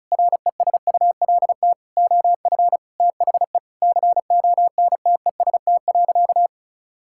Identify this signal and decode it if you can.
Morse code, heard as a keyed tone, about 36 wpm, 700 Hz. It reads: RESULT OF THE CONTEST.
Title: Sentences from Top 1600 Words 35wpm